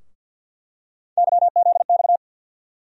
Morse code, heard as a keyed tone, about 47 wpm, 700 Hz. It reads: YZX
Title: USA Call Sign Postfixes - 3 Letters 50wpm